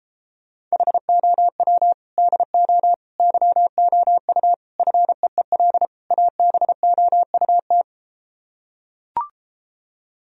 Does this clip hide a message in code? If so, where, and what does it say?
Yes: Morse code from 0.7 to 7.8 s, heard as a keyed tone, about 33 wpm, 700 Hz. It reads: HOW DO YOU FEEL ABOUT